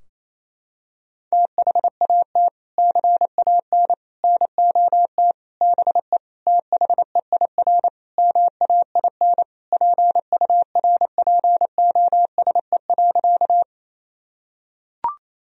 Morse code, heard as a keyed tone, about 28 wpm, 700 Hz. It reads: THAT CAN NOT BE THEIR MAIN PURPOSE.
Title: Sentences from Top 1000 Words - Review Part 2 - 28wpm